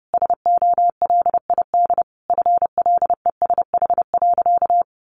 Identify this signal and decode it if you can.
Morse code, heard as a keyed tone, about 30 wpm, 700 Hz. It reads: SOLID FLESH.
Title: A Sound Of Thunder 30wpm